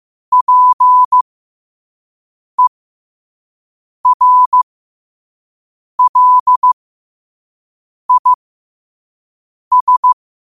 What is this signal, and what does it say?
Morse code, heard as a keyed tone, about 15 wpm, spaced as for 5 wpm, 1000 Hz. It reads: PERLIS
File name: qotdfiles / 5wpm